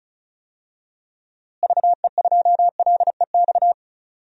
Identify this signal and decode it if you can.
Morse code, heard as a keyed tone, about 36 wpm, 700 Hz. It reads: VE2LEX
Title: DXCC-100 Call Signs - 6 Characters - Repeated 2x - 35wpm